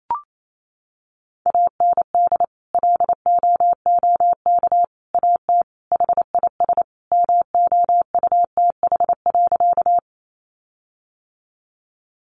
Morse code, heard as a keyed tone, about 28 wpm, 700 Hz. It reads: AND LOOK AT HIS MOUTH.
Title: Sentences from Top 1500 Words 28wpm